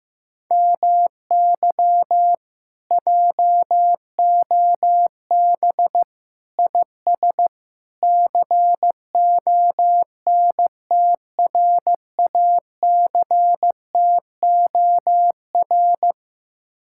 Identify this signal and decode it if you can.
Morse code, heard as a keyed tone, about 15 wpm, 700 Hz. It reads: MY JOB IS CONTRACTOR
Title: Speed-Racing Common QSO Phrases Encore - 15wpm